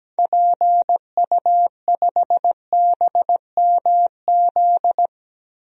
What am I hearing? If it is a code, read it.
Morse code, heard as a keyed tone, about 17 wpm, 700 Hz. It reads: PU5BMZ
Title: DXCC-100 Call Signs - 6 Characters - Repeated 3x - 17wpm